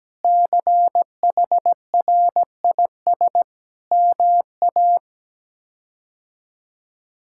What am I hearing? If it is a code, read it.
Morse code, heard as a keyed tone, about 17 wpm, 700 Hz. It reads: CHRIS MA
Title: K1USN SST QSO Practice - 17wpm